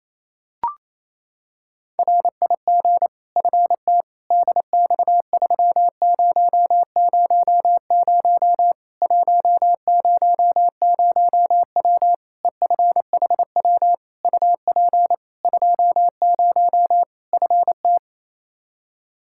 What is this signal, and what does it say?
Morse code, heard as a keyed tone, about 28 wpm, 700 Hz. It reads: RIG FT DX3000 100W EFHW UP 20 FT
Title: K1USN SST QSO Practice 28wpm